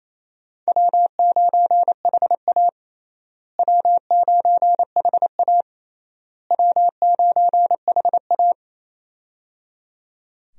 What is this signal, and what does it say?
Morse code, heard as a keyed tone, about 28 wpm, 700 Hz. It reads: W9HA W9HA W9HA
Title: USA Call Signs - 4 Characters - Repeated 3x - 28wpm